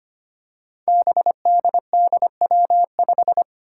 Morse code, heard as a keyed tone, about 25 wpm, 700 Hz. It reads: BDDW5
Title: Five Challenging Letter-Number Combinations 25wpm